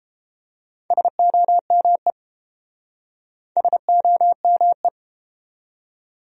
Morse code, heard as a keyed tone, about 30 wpm, 700 Hz. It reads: SOME SOME